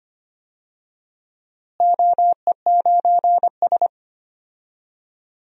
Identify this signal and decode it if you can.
Morse code, heard as a keyed tone, about 25 wpm, 700 Hz. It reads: OE9S
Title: DXCC-100 Call Signs - 4 Characters - Repeated 3x - 25wpm